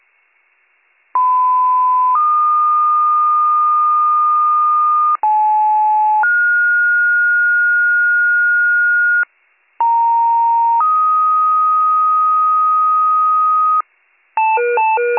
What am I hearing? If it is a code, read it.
Two-tone page, heard as a one-second tone, then a three-second tone, about 1010 then 1220 Hz, about 860 then 1430 Hz, about 930 then 1250 Hz. It is heard from 1.1 to 13.8 s.